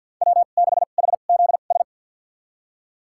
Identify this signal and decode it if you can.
Morse code, heard as a keyed tone, about 51 wpm, spaced as for 36 wpm, 700 Hz. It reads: W6HZS
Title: USA Call Signs - 2x Character Spacing - 50wpm